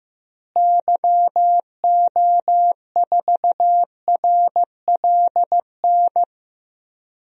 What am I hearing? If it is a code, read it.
Morse code, heard as a keyed tone, about 15 wpm, 700 Hz. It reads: YO4RLN
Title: DXCC-100 Call Signs - 6 Characters - Repeated 2x - 15wpm